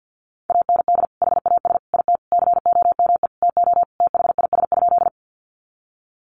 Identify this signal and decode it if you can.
Morse code, heard as a keyed tone, about 50 wpm, 700 Hz. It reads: AND HAS IT COME TO THIS?
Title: Sentences from Top 100 Words 50wpm